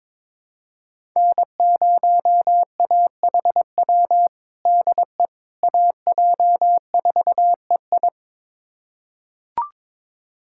Morse code, heard as a keyed tone, about 22 wpm, 700 Hz. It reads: N0AHW DE AJ4EI
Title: USA Callsign DE Callsign - 5 Characters - 22wpm